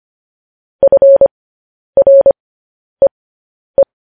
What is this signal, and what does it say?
Morse code, heard as a keyed tone, about 25 wpm, spaced as for 10 wpm, 550 Hz. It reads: FREE
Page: News Headlines in Morse Code at 10 WPM